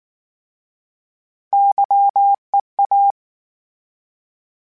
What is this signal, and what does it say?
Morse code, heard as a keyed tone, about 19 wpm, 800 Hz. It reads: YEA